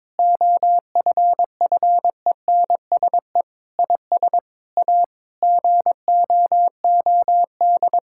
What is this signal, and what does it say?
Morse code, heard as a keyed tone, about 22 wpm, 700 Hz. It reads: OFFENSE IS A GOOD
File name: Difficult Sentences with Hints - 22wpm